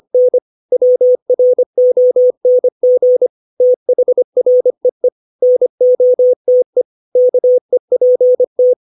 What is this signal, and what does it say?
Morse code, heard as a keyed tone, about 25 wpm, 500 Hz. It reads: N WRONG THREE NOTE KEPT
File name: 632 words / 632 Words 25WPM